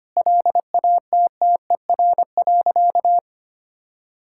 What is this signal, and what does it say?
Morse code, heard as a keyed tone, about 25 wpm, 700 Hz. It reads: LATTER.
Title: Sentences from Top 1600 Words 25wpm